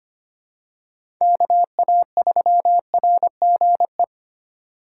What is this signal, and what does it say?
Morse code, heard as a keyed tone, about 25 wpm, 700 Hz. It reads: KA3RGE
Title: USA Call Signs - 6 Characters - 3x Character Spacing - 25wpm